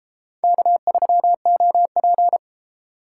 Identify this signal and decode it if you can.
Morse code, heard as a keyed tone, about 32 wpm, 700 Hz. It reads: K3OP